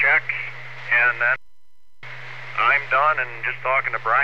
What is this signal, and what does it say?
Emergency Alert System (SAME) tones, heard as harsh buzzing bursts, with speech. It is heard from 0.9 to 3.6 s.